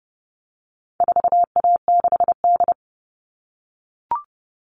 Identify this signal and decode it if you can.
Morse code, heard as a keyed tone, about 30 wpm, 700 Hz. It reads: 4A6D
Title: All Cut Numbers - 4-Digit 30wpm